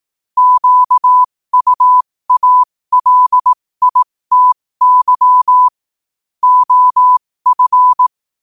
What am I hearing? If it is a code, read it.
Morse code, heard as a keyed tone, about 18 wpm, spaced as for 15 wpm, 1000 Hz. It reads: QUALITY OF
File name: qotdfiles / 15wpm